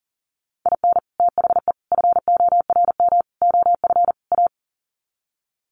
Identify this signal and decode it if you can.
Morse code, heard as a keyed tone, about 40 wpm, 700 Hz. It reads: IN THE FORM OF A
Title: Sets of 5 Words 40wpm